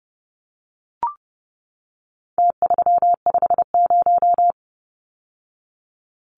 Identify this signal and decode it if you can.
Morse code, heard as a keyed tone, about 30 wpm, 700 Hz. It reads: T350
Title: Numbers - Contest Style - 4-Digit 30wpm